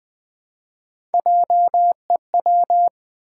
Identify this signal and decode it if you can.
Morse code, heard as a keyed tone, about 20 wpm, 700 Hz. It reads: JEW
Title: Build A Word - 3 Letters At A Time - 9-Letter Words - 20wpm